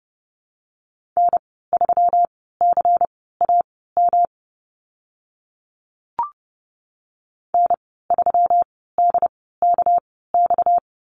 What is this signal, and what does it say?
Morse code, heard as a keyed tone, about 30 wpm, spaced as for 17 wpm, 700 Hz. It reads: N3CAM N3DKX
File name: Lesson 37g - Letter x - Call Signs 30wpm